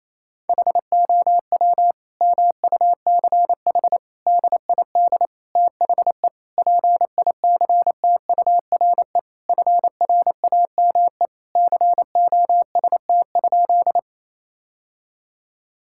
Morse code, heard as a keyed tone, about 28 wpm, 700 Hz. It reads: HOW MUCH DID THE PICTURE FRAME COST?
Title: Sentences from Top 1000 Words - Review Part 6 - 28wpm